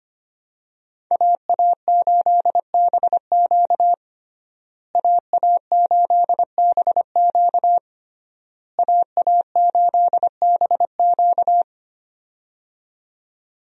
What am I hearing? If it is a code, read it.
Morse code, heard as a keyed tone, about 25 wpm, 700 Hz. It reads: AA8BQ AA8BQ AA8BQ
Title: USA Call Signs - 5 Characters - Repeated 3x - 25wpm